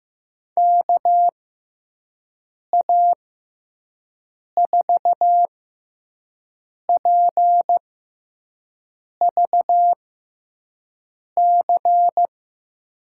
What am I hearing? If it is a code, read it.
Morse code, heard as a keyed tone, about 15 wpm, spaced as for 5 wpm, 700 Hz. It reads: KA4PVC
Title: USA Call Signs - 6 Characters - 6x Character Spacing - 15wpm